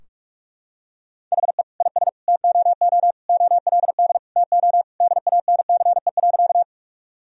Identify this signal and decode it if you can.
Morse code, heard as a keyed tone, about 46 wpm, 700 Hz. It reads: HE IS TOO OLD TO DANCE.
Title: Sentences from Top 1000 Words - Review Part 4 - 45wpm